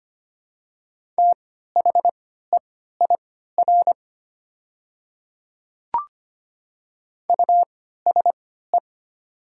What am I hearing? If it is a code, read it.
Morse code, heard as a keyed tone, about 25 wpm, spaced as for 14 wpm, 700 Hz. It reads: THEIR USE
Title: 3x Character Spacing - Top 200 Words - 25wpm